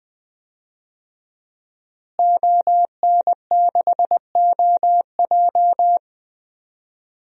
Morse code, heard as a keyed tone, about 21 wpm, 700 Hz. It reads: ON6OJ
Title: DXCC-100 Call Signs - 5 Characters - Repeated 3x - 20wpm